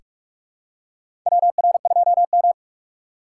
Morse code, heard as a keyed tone, about 48 wpm, 700 Hz. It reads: WR2M
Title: USA Call Signs - 6x Character Spacing - 45wpm